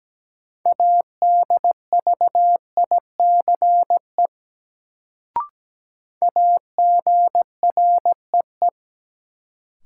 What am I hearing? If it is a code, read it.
Morse code, heard as a keyed tone, about 17 wpm, 700 Hz. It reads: ADVICE AGREE
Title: Sentences from Top 1100 Words - New Word Drill Alt - 17wpm